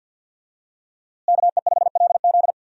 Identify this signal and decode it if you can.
Morse code, heard as a keyed tone, about 50 wpm, 700 Hz. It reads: KE5LZ
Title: USA Call Signs - 5 Characters - Repeated 3x - 50wpm